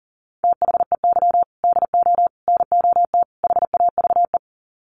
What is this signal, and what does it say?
Morse code, heard as a keyed tone, about 40 wpm, 700 Hz. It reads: THEY DO NOT HAVE
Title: Sets of 4 Words 40wpm